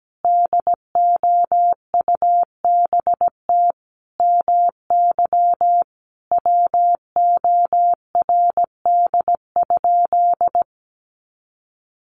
Morse code, heard as a keyed tone, about 17 wpm, 700 Hz. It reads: DOUBT MY WORD?
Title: Sentences from Top 1500 Words 17wpm